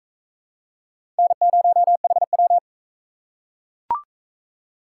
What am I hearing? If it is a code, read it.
Morse code, heard as a keyed tone, about 36 wpm, 700 Hz. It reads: N0HW